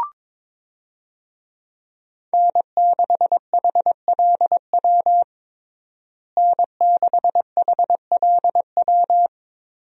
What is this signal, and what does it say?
Morse code, heard as a keyed tone, about 22 wpm, 700 Hz. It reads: N6HLW N6HLW
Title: USA Call Signs - 5 Characters - Repeated 3x - 22wpm